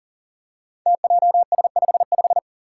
Morse code, heard as a keyed tone, about 37 wpm, 700 Hz. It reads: TJS55
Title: Five Challenging Letter-Number Combinations 40wpm